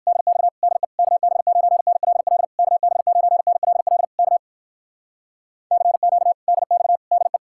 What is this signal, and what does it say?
Morse code, heard as a keyed tone, about 58 wpm, 700 Hz. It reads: DX DE KB9NLL KB9NLL K CQ DX DE